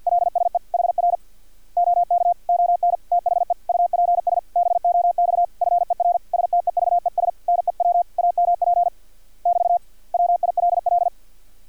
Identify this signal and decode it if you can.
Morse code, heard as a keyed tone, about 50 wpm, 700 Hz. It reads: LSE HR OK ON THE UPS BOX FER STEVES NEW AMP <BT> WILL
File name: archived / CRSnet